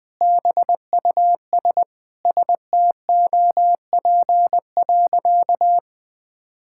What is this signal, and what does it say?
Morse code, heard as a keyed tone, about 20 wpm, 700 Hz. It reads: BUS STOP.
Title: Sentences with 2-Word Phrases - Spoken Beginning - 20wpm